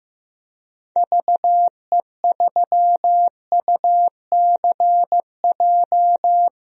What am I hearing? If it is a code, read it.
Morse code, heard as a keyed tone, about 15 wpm, 700 Hz. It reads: VE3UCJ
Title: DXCC-100 Call Signs - 6 Characters - Repeated 3x - 15wpm